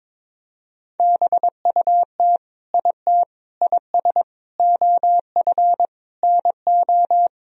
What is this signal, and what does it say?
Morse code, heard as a keyed tone, about 22 wpm, 700 Hz. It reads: BUT IT IS OF NO
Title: Sentences from Top 1000 Words - Review Part 2 - 22wpm